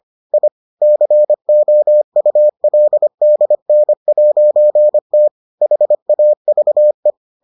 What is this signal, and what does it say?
Morse code, heard as a keyed tone, about 25 wpm, 600 Hz. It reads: I COULDN'T HAVE